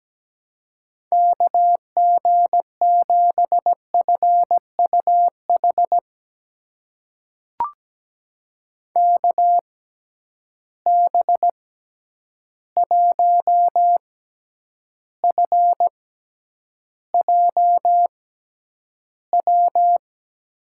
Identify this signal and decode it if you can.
Morse code, heard as a keyed tone, about 17 wpm, spaced as for 9 wpm, 700 Hz. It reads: KG7FUH KB1FJW
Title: USA Call Signs - 6 Characters - 6x Character Spacing - 17wpm